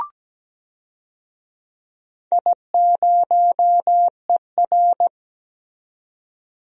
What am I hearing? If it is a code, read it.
Morse code, heard as a keyed tone, about 17 wpm, 700 Hz. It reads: I0ER
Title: DXCC-100 Call Signs - 4 Characters - 17wpm